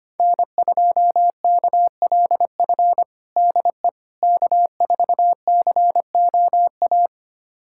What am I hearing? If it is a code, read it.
Morse code, heard as a keyed tone, about 25 wpm, 700 Hz. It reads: N2KLF DE K4COA
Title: USA Callsign DE Callsign - 5 Characters - 25wpm